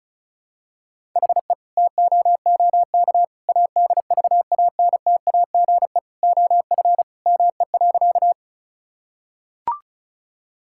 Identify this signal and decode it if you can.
Morse code, heard as a keyed tone, about 36 wpm, 700 Hz. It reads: HE TOOK ADVANTAGE OF ME.
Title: Sentences from Top 1000 Words - Review Part 6 - 35wpm